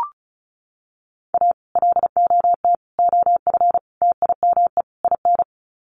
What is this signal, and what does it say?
Morse code, heard as a keyed tone, about 34 wpm, 700 Hz. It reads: A LOT OF TIME IN
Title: Sets of 5 Words Encore2 35wpm